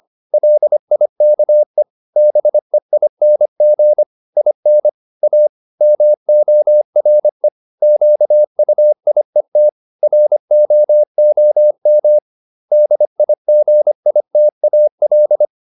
Morse code, heard as a keyed tone, about 25 wpm, 600 Hz. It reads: LIKE BEING IN A MORE QUIET ROOM DIGITAL